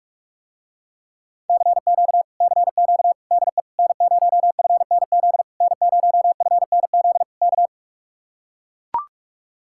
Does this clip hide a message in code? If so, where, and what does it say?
Yes: Morse code from 1.5 to 7.7 s, heard as a keyed tone, about 49 wpm, 700 Hz. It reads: CQ CQ DE N0FNZ N0FNZ K